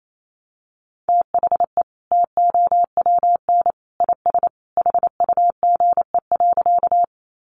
Morse code, heard as a keyed tone, about 28 wpm, 700 Hz. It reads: THE TOWN IS HUGE.
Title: Sentences from Top 1100 Words 28wpm